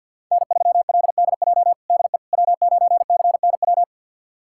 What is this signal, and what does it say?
Morse code, heard as a keyed tone, about 52 wpm, 700 Hz. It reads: N3LDJ DE W9CNW
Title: USA Callsign DE Callsign - 5 Characters - 50wpm